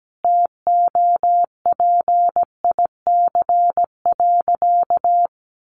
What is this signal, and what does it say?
Morse code, heard as a keyed tone, about 17 wpm, 700 Hz. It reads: TOPIC.
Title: Sentences from Top 1200 Words 17wpm